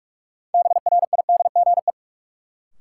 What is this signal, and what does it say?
Morse code, heard as a keyed tone, about 46 wpm, 700 Hz. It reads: BRIDGE
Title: Sentences from Top 1200 Words - New Word Drill Alt - 45wpm